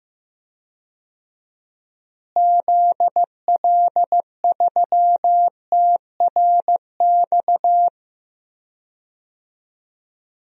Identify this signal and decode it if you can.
Morse code, heard as a keyed tone, about 15 wpm, 700 Hz. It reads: ZL3TRX
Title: DXCC-100 Call Signs - 6 Characters - Repeated 3x - 15wpm